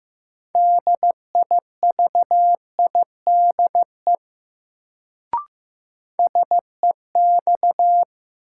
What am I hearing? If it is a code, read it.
Morse code, heard as a keyed tone, about 15 wpm, 700 Hz. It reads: DIVIDE SEX
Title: Sentences from Top 1600 Words - New Word Drill - 15wpm